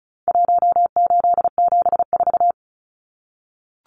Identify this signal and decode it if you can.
Morse code, heard as a keyed tone, about 34 wpm, 700 Hz. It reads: 1874
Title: Numbers - Contest Style - 4-Digit 35wpm